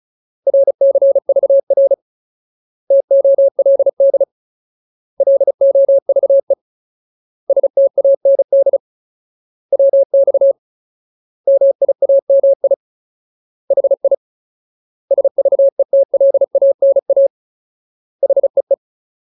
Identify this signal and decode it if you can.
Morse code, heard as a keyed tone, about 34 wpm, 550 Hz. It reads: RCVR TOLD LOVE STAND WX MIAMI HI SVETLANA HEE
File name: Ultimate Word List / 35WPM Ultimate